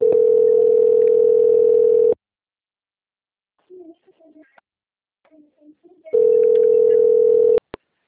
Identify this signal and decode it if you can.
Ringback tone (North America), 440 + 480 Hz, with speech.
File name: out- - - - - .41635